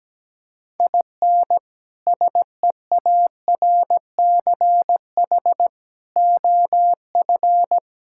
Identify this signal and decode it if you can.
Morse code, heard as a keyed tone, about 17 wpm, 700 Hz. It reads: IN SEARCH OF
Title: Sentences from Top 1000 Words - Review Part 5 - 17wpm